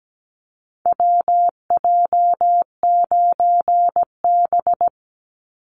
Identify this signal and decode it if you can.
Morse code, heard as a keyed tone, about 17 wpm, 700 Hz. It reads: WJ9B
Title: Call Signs - 17wpm